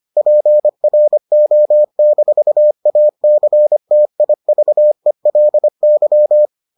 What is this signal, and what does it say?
Morse code, heard as a keyed tone, about 25 wpm, 600 Hz. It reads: PRO-ACTIVELY